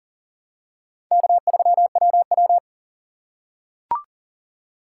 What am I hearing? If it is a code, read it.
Morse code, heard as a keyed tone, about 40 wpm, 700 Hz. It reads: K3WW
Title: USA Call Signs - 4 Characters - 3x Character Spacing - 40wpm